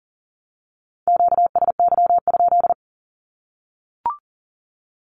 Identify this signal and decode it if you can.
Morse code, heard as a keyed tone, about 40 wpm, 700 Hz. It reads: QSY?
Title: Q-Codes 40wpm